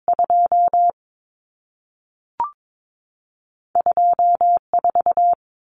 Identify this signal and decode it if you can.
Morse code, heard as a keyed tone, about 22 wpm, spaced as for 3 wpm, 700 Hz. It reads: 22 4
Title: Common Cut Numbers - 2-Digit 22wpm